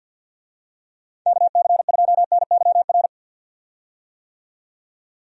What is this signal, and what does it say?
Morse code, heard as a keyed tone, about 51 wpm, 700 Hz. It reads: KC2NYR
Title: USA Call Signs - 6 Characters - 50wpm